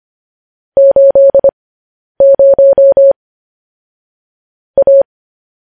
Morse code, heard as a keyed tone, about 25 wpm, spaced as for 10 wpm, 550 Hz. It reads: 80 A
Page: News Headlines in Morse Code at 10 WPM